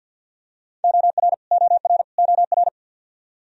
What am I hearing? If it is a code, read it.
Morse code, heard as a keyed tone, about 51 wpm, 700 Hz. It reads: OR OR OR
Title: Top 100 Words Repeated 50wpm